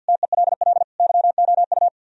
Morse code, heard as a keyed tone, about 62 wpm, 700 Hz. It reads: TELL YOU
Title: Speed-Racing - Sets of 4 Words 50wpm